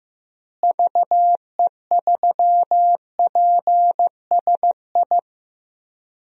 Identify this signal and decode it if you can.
Morse code, heard as a keyed tone, about 15 wpm, 700 Hz. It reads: VE3PSI
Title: DXCC-100 Call Signs - 6 Characters - Repeated 2x - 15wpm